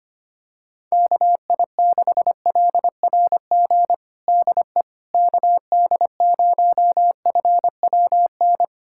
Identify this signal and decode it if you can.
Morse code, heard as a keyed tone, about 25 wpm, 700 Hz. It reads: KI6LRG DE KD0FWN
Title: USA Callsign DE Callsign - 6 Characters - 25wpm